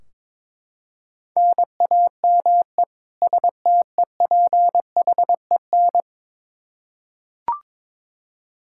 Morse code, heard as a keyed tone, about 22 wpm, 700 Hz. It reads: NAME STEPHEN
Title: Practice QSO - Exchange 1 - Beginning - 22wpm